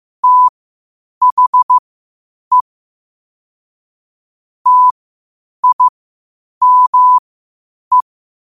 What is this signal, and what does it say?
Morse code, heard as a keyed tone, about 15 wpm, spaced as for 8 wpm, 1000 Hz. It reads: THE TIME
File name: qotdfiles / 8wpm